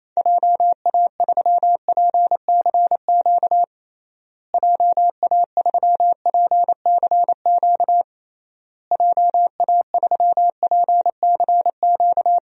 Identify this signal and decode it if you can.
Morse code, heard as a keyed tone, about 28 wpm, 700 Hz. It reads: JA3PCQ JA3PCQ JA3PCQ
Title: DXCC-100 Call Signs - 6 Characters - Repeated 3x - 28wpm